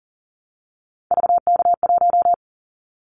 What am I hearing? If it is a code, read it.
Morse code, heard as a keyed tone, about 42 wpm, 700 Hz. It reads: VK1
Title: Callsign Prefixes for Top 50 DXCC - 40wpm